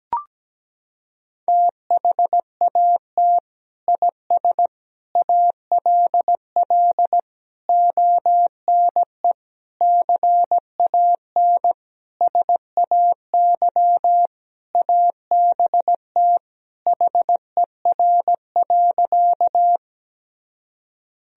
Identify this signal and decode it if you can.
Morse code, heard as a keyed tone, about 17 wpm, 700 Hz. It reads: THAT IS ALL ONE CAN SAY ABT HER.
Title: Sentences from Top 200 Words and Abbreviations 17wpm